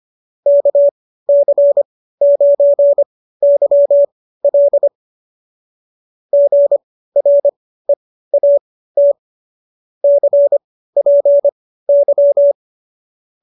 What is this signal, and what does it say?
Morse code, heard as a keyed tone, about 25 wpm, spaced as for 15 wpm, 590 Hz. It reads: KC9YL GREAT CPY